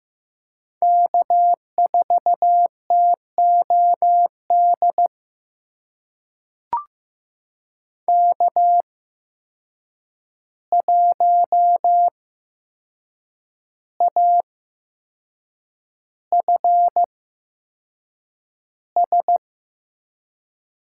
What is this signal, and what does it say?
Morse code, heard as a keyed tone, about 15 wpm, spaced as for 6 wpm, 700 Hz. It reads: K4TOD K1AFS
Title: USA Call Signs - 5 Characters - 8x Character Spacing - 15wpm